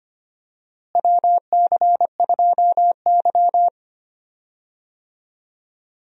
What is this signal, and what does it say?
Morse code, heard as a keyed tone, about 25 wpm, 700 Hz. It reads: WC2Y
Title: USA Call Signs - 4 Characters - Repeated 2x - 25wpm